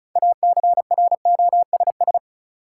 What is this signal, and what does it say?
Morse code, heard as a keyed tone, about 34 wpm, 700 Hz. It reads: ACROSS
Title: Sentences from Top 700 Words - New Word Drill - 35wpm